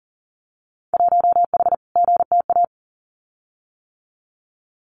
Morse code, heard as a keyed tone, about 40 wpm, 700 Hz. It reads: 1H GTA
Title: Winter Field Day Exchanges 40wpm